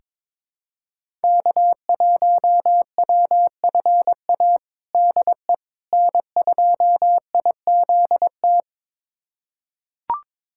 Morse code, heard as a keyed tone, about 21 wpm, 700 Hz. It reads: K1WFA DE N2IZT
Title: USA Callsign DE Callsign - 5 Characters - 22wpm